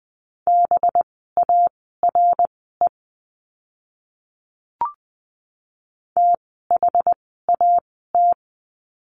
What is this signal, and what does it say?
Morse code, heard as a keyed tone, about 20 wpm, spaced as for 14 wpm, 700 Hz. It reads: BARE THAT
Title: 185 Lesson 30f - Letter b - Words 20wpm